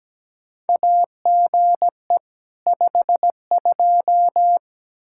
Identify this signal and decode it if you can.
Morse code, heard as a keyed tone, about 17 wpm, 700 Hz. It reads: AGE 52
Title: Practice QSO - Exchange 3 - More Info - 17wpm